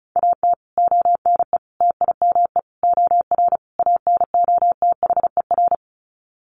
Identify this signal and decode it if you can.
Morse code, heard as a keyed tone, about 34 wpm, 700 Hz. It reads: AT ONE TIME OR ANOTHER
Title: Sets of 5 Words Encore 35wpm